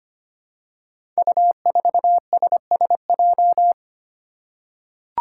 Morse code, heard as a keyed tone, about 25 wpm, 700 Hz. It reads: U4SSJ
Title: Five Challenging Letter-Number Combinations 25wpm